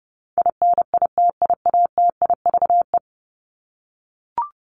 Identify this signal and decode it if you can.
Morse code, heard as a keyed tone, about 30 wpm, 700 Hz. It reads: INITIATIVE
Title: Top 1751-2000 Words 30wpm